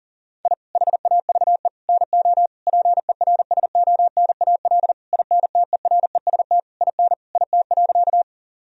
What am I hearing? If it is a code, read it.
Morse code, heard as a keyed tone, about 40 wpm, 700 Hz. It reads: I HAVE NO PERSONAL INTEREST IN IT.
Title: Sentences from Top 1000 Words - Review Part 4 - 40wpm